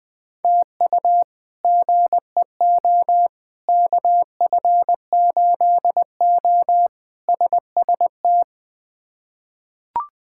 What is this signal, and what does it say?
Morse code, heard as a keyed tone, about 20 wpm, 700 Hz. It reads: TU GEO KF8O SST
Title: K1USN SST QSO Practice - 20wpm